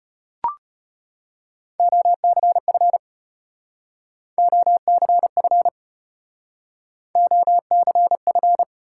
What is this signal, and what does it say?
Morse code, heard as a keyed tone, about 31 wpm, 700 Hz. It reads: OCF OCF OCF